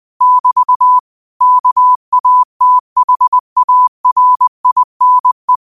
Morse code, heard as a keyed tone, about 20 wpm, 1000 Hz. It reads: <BT> KATHARINE